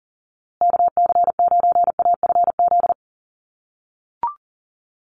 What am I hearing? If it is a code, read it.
Morse code, heard as a keyed tone, about 40 wpm, 700 Hz. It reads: KC9AFZ
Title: USA Call Signs Repeated 3x 40wpm